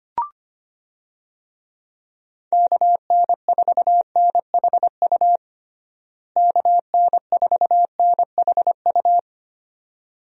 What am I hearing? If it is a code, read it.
Morse code, heard as a keyed tone, about 25 wpm, 700 Hz. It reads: KN4NHU KN4NHU
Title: USA Call Signs - 6 Characters - Repeated 2x - 25wpm